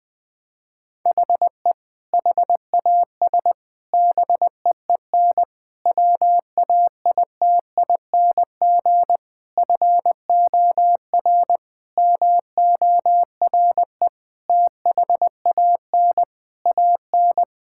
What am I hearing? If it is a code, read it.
Morse code, heard as a keyed tone, about 21 wpm, 700 Hz. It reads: HE HAS BEEN WAITING FOR MORE THAN AN
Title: Sentences with 3-Word Phrases - 20wpm